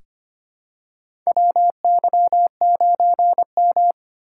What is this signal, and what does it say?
Morse code, heard as a keyed tone, about 25 wpm, 700 Hz. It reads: WY9M
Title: USA Call Signs - 4 Characters - 3x Character Spacing - 25wpm